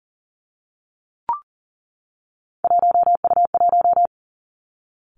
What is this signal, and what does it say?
Morse code, heard as a keyed tone, about 38 wpm, 690 Hz. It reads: 1U1
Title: Three Challenging Letter-Number Combinations 40wpm